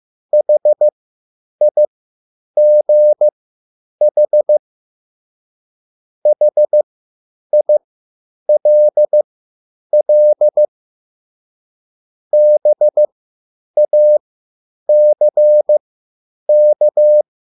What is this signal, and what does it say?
Morse code, heard as a keyed tone, about 15 wpm, spaced as for 9 wpm, 600 Hz. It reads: HIGH HILL BACK
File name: QSL / CW lesson